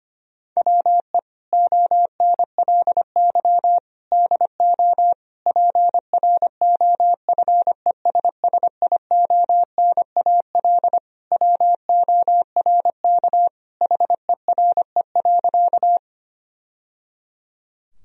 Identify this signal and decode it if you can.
Morse code, heard as a keyed tone, about 25 wpm, 700 Hz. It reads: WE ONLY DO PROFESSIONAL WORK HERE.
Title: Sentences from Top 1000 Words - Review Part 4 - 25wpm